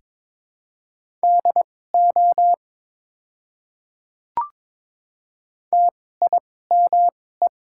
Morse code, heard as a keyed tone, about 22 wpm, spaced as for 16 wpm, 690 Hz. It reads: DO TIME
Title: 2x Character Spacing - Top 100 Words - 22wpm